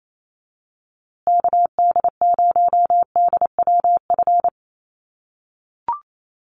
Morse code, heard as a keyed tone, about 28 wpm, 700 Hz. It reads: KD0DWF
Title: USA Call Signs Repeated 2x 28wpm